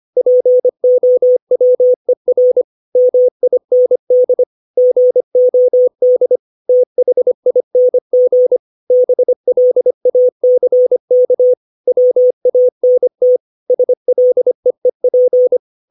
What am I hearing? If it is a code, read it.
Morse code, heard as a keyed tone, about 25 wpm, 500 Hz. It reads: POWER MIND GOD THING BLACK WANT SLEEP
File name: 632 words / 632 Words 25WPM